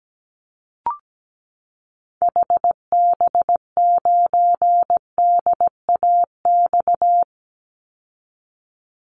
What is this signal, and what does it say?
Morse code, heard as a keyed tone, about 17 wpm, 700 Hz. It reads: HB9DAX
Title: Call Signs Repeated 2x - 17wpm